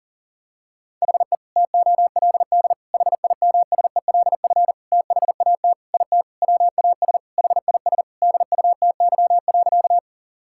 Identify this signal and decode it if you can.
Morse code, heard as a keyed tone, about 40 wpm, 700 Hz. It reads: HE TOLD HIMSELF THAT IT WAS HIS DUTY.
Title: Sentences from Top 1600 Words 40wpm